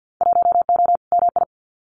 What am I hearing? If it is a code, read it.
Morse code, heard as a keyed tone, about 50 wpm, 700 Hz. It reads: 1O MI
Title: Winter Field Day Exchanges Encore 50wpm